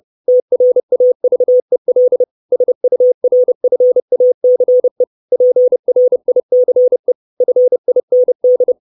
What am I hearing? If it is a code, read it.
Morse code, heard as a keyed tone, about 30 wpm, 500 Hz. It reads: TRAVEL SURFACE PRICE FIND
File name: 632 Words 30WPM